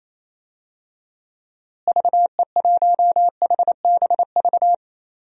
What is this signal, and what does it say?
Morse code, heard as a keyed tone, about 28 wpm, 700 Hz. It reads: VE1HBV
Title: DXCC-100 Call Signs - 6 Characters - Repeated 3x - 28wpm